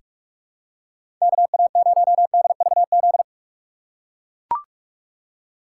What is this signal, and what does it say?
Morse code, heard as a keyed tone, about 42 wpm, 700 Hz. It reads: KA0DVZ
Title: USA Call Signs - 6 Characters - 3x Character Spacing - 45wpm